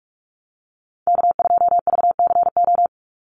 Morse code, heard as a keyed tone, about 44 wpm, 700 Hz. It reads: K2VCO
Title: Call Signs - 45wpm